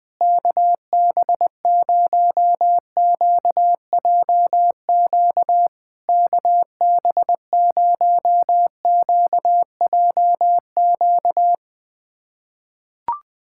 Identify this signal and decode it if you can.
Morse code, heard as a keyed tone, about 21 wpm, 700 Hz. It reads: KB0QJQ KB0QJQ
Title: SOTA QSO Practice 20wpm